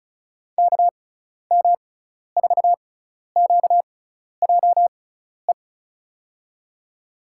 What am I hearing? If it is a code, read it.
Morse code, heard as a keyed tone, about 36 wpm, spaced as for 12 wpm, 700 Hz. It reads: KM4QJE
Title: USA Call Signs - 6 Characters - 6x Character Spacing - 35wpm